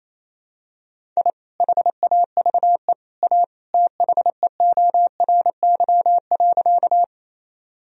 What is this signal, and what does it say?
Morse code, heard as a keyed tone, about 28 wpm, 700 Hz. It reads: I HAVE A THEORY.
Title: Sentences from Top 1000 Words - Review Part 3 - 28wpm